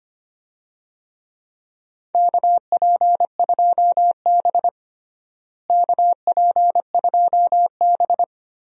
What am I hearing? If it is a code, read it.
Morse code, heard as a keyed tone, about 25 wpm, 700 Hz. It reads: KP2B KP2B
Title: DXCC-100 Call Signs - 4 Characters - Repeated 3x - 25wpm